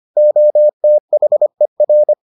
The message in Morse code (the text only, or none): OTHER